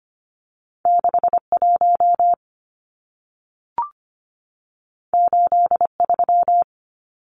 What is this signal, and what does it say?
Morse code, heard as a keyed tone, about 25 wpm, 700 Hz. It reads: 61 83
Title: Numbers - 2-Digit 25wpm